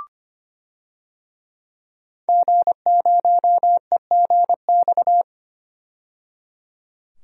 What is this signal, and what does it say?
Morse code, heard as a keyed tone, about 25 wpm, 700 Hz. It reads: G0EGX
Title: DXCC-100 Call Signs - 5 Characters - 25wpm